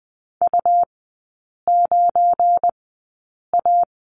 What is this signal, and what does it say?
Morse code, heard as a keyed tone, about 20 wpm, spaced as for 8 wpm, 700 Hz. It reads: U9A
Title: 184 Lesson 30e - Letter b - Three Characters 20wpm